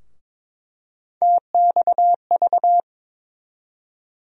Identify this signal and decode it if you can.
Morse code, heard as a keyed tone, about 22 wpm, 700 Hz. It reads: TXV
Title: USA Call Sign Postfixes - 3 Letters 22wpm